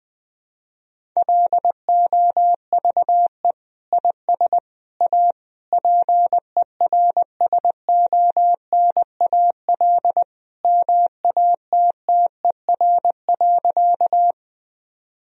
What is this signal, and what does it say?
Morse code, heard as a keyed tone, about 20 wpm, 700 Hz. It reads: LOVE IS A PERSONAL MATTER.
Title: Sentences from Top 1000 Words - Review Part 4 - 20wpm